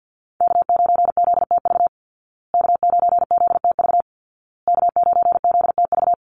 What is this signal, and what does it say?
Morse code, heard as a keyed tone, about 48 wpm, 700 Hz. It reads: K9ZTV K9ZTV K9ZTV
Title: Call Signs Repeated 2x - 50wpm